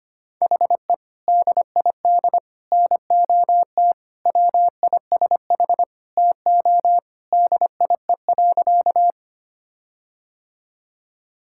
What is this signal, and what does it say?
Morse code, heard as a keyed tone, about 25 wpm, 700 Hz. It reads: HE DID NOT WISH TO DIE.
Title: Sentences from Top 1600 Words 25wpm